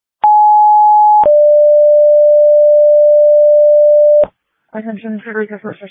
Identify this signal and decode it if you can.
Two-tone page, heard as a one-second tone, then a three-second tone, about 840 then 590 Hz, with speech. It is heard from 0.2 to 4.2 s.